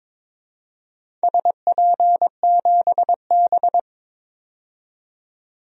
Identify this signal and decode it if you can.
Morse code, heard as a keyed tone, about 22 wpm, 700 Hz. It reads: SP7B
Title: DXCC-100 Call Signs - 4 Characters - 22wpm